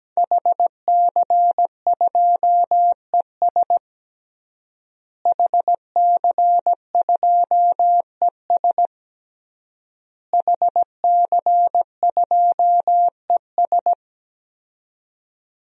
Morse code, heard as a keyed tone, about 17 wpm, 700 Hz. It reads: HC2ES HC2ES HC2ES
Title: DXCC-100 Call Signs - 5 Characters - Repeated 3x - 17wpm